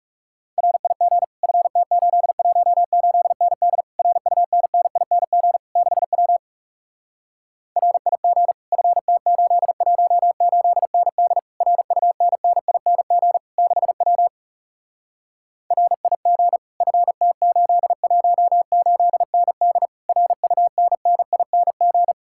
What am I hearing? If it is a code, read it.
Morse code, heard as a keyed tone, about 40 wpm, 700 Hz. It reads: RIG FT818ND RUNNING 6W RIG FT818ND RUNNING 6W RIG FT818ND RUNNING